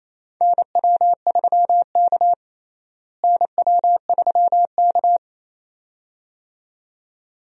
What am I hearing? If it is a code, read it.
Morse code, heard as a keyed tone, about 28 wpm, 700 Hz. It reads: NW3K NW3K
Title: USA Call Signs - 4 Characters - Repeated 2x - 28wpm